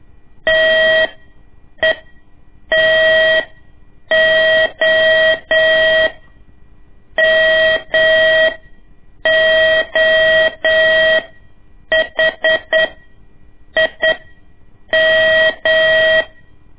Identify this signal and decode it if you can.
Morse code, heard as a keyed tone, about 9 wpm, spaced as for 7 wpm, 640 Hz. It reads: TETO MOHIM
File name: ik1weg2 / CORSI CW / lez2-1